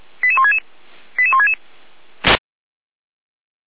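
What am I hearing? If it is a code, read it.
Five-tone selective call (ZVEI), heard as quick runs of five tones, tones of about 70 ms. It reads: 88166 88166
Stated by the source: ZVEI five-tone call 88247 received by SDR, feuerwehr-orlamuende.org fire brigade audio archive.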